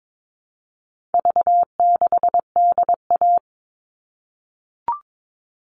Morse code, heard as a keyed tone, about 22 wpm, 700 Hz. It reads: V6DA
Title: All Cut Numbers - 4-Digit 22wpm